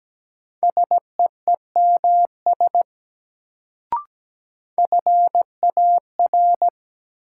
Morse code, heard as a keyed tone, about 17 wpm, 700 Hz. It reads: SEEMS FAR
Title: Sets of 3 Words - New Word Drill - 17wpm